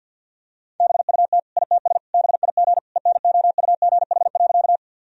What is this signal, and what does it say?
Morse code, heard as a keyed tone, about 51 wpm, 700 Hz. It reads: BUT ITS BIG ENOUGH.
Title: Sentences from Top 1000 Words - Review Part 1 - 50wpm